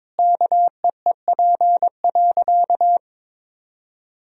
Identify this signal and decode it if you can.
Morse code, heard as a keyed tone, about 22 wpm, 700 Hz. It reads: KEEP.
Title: Sentences from Top 1000 Words - Review Part 2 - 22wpm